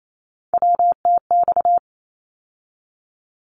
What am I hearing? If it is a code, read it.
Morse code, heard as a keyed tone, about 28 wpm, 700 Hz. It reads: WTX
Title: ARRL Sections 28wpm